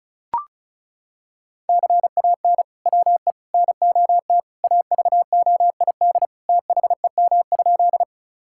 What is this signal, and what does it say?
Morse code, heard as a keyed tone, about 36 wpm, 700 Hz. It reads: CAN WE NOT AVOID THEM?
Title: Sentences from Top 1000 Words - Review Part 6 - 35wpm